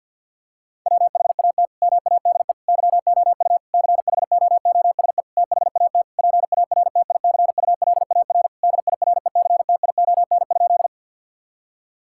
Morse code, heard as a keyed tone, about 50 wpm, 700 Hz. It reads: WHAT MADE YOU CHOOSE THAT PARTICULAR DIRECTION?
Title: Sentences from Top 1000 Words - Review Part 5 - 50wpm